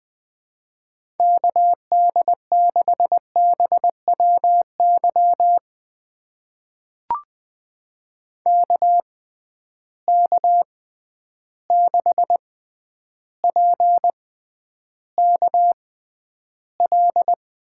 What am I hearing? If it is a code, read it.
Morse code, heard as a keyed tone, about 21 wpm, spaced as for 10 wpm, 700 Hz. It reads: KD6BWY KK6PKL
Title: USA Call Signs - 6 Characters - 6x Character Spacing - 20wpm